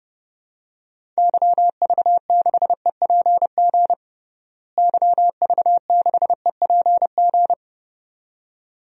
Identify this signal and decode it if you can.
Morse code, heard as a keyed tone, about 29 wpm, 700 Hz. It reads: YV6EPG YV6EPG
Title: DXCC-100 Call Signs - 6 Characters - Repeated 2x - 30wpm